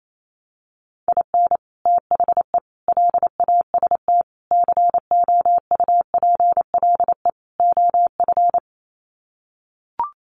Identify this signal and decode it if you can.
Morse code, heard as a keyed tone, about 28 wpm, 700 Hz. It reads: IN THE LAST COUPLE OF
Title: Sets of 5 Words 28wpm